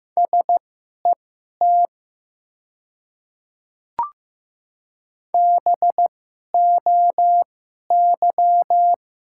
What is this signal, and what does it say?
Morse code, heard as a keyed tone, about 15 wpm, spaced as for 11 wpm, 700 Hz. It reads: SET BOY
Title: 2x Character Spacing - Top 200 Words - 15wpm